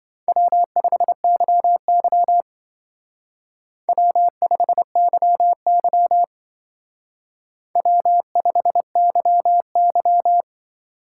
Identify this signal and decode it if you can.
Morse code, heard as a keyed tone, about 26 wpm, 700 Hz. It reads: W5YY W5YY W5YY